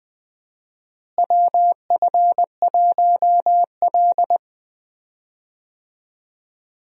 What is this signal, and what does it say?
Morse code, heard as a keyed tone, about 20 wpm, 700 Hz. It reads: WF1L
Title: USA Call Signs - 4 Characters - Repeated 3x - 20wpm